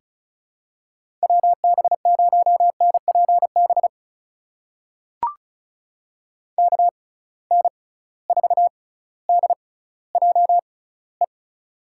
Morse code, heard as a keyed tone, about 36 wpm, spaced as for 18 wpm, 700 Hz. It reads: WB0NPB KN4DJE
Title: USA Call Signs - 6 Characters - 6x Character Spacing - 35wpm